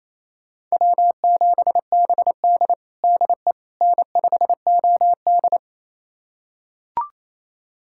Morse code, heard as a keyed tone, about 28 wpm, 700 Hz. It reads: W7BD DE N5OD
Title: USA Callsign DE Callsign - 4 Characters - 28wpm